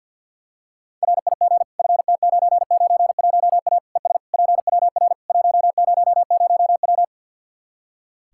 Morse code, heard as a keyed tone, about 46 wpm, 700 Hz. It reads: RIG FT991A ES PWR 100W
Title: K1USN SST QSO Practice 50wpm